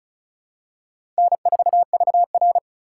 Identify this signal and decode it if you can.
Morse code, heard as a keyed tone, about 36 wpm, 700 Hz. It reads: N4VR
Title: USA Call Signs - 4 Characters - 2x Character Spacing - 35wpm